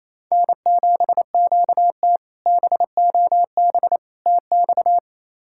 Morse code, heard as a keyed tone, about 28 wpm, 700 Hz. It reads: N7QT BOB TX
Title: K1USN SST QSO Practice - 28wpm